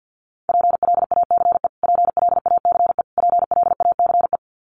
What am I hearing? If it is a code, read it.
Morse code, heard as a keyed tone, about 50 wpm, 700 Hz. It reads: PLACE PLACE PLACE
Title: Top 500 Words Repeated 50wpm